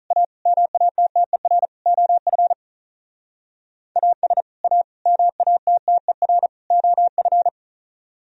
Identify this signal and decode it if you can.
Morse code, heard as a keyed tone, about 45 wpm, 700 Hz. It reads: A MATTER OF AS A MATTER OF